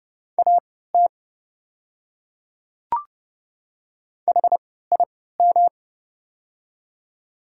Morse code, heard as a keyed tone, about 30 wpm, spaced as for 17 wpm, 700 Hz. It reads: AT HIM
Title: 3x Character Spacing - Top 100 Words - 30wpm